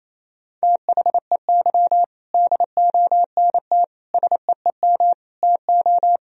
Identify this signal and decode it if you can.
Morse code, heard as a keyed tone, about 27 wpm, 700 Hz. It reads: THEY DONT SEEM TO
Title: Sentences from Top 1000 Words - Review Part 1 - 28wpm